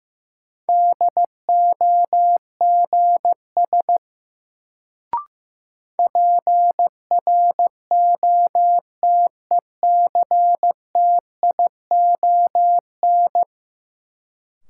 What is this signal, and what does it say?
Morse code, heard as a keyed tone, about 15 wpm, 700 Hz. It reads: DOGS PROTECTION
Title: Sentences from Top 1600 Words - New Word Drill - 15wpm